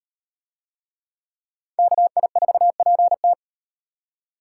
Morse code, heard as a keyed tone, about 38 wpm, 700 Hz. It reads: KI4PT